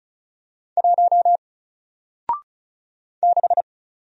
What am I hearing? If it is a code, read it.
Morse code, heard as a keyed tone, about 36 wpm, 700 Hz. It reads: 1 6
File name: Single-Letter-Number-35wpm